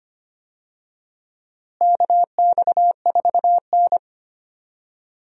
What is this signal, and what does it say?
Morse code, heard as a keyed tone, about 25 wpm, 700 Hz. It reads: KX4N
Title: USA Call Signs - 4 Characters - Repeated 2x - 25wpm